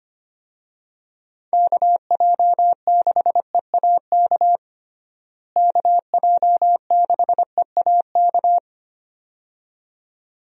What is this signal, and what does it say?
Morse code, heard as a keyed tone, about 25 wpm, 700 Hz. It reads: KJ6EAK KJ6EAK
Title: USA Call Signs - 6 Characters - Repeated 2x - 25wpm